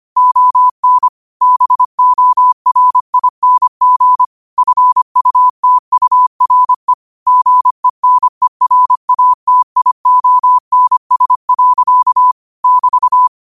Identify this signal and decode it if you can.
Morse code, heard as a keyed tone, about 25 wpm, 1000 Hz. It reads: ON BORING FUTURE GENERATIONS. <BT>
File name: 25wpm